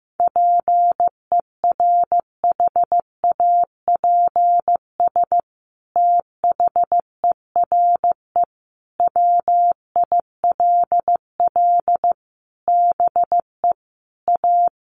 Morse code, heard as a keyed tone, about 15 wpm, 700 Hz. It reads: PERHAPS THERE WILL BE A
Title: Sentences from Top 1400 Words 15wpm